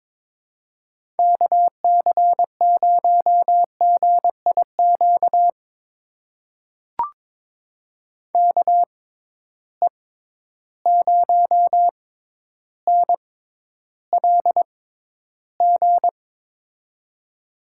Morse code, heard as a keyed tone, about 22 wpm, spaced as for 11 wpm, 700 Hz. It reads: KC0GIQ KE0NLG
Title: USA Call Signs - 6 Characters - 6x Character Spacing - 22wpm